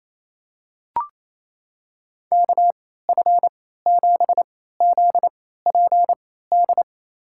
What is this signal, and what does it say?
Morse code, heard as a keyed tone, about 28 wpm, spaced as for 16 wpm, 700 Hz. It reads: KF7ZPD
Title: USA Call Signs - 3x Character Spacing - 28wpm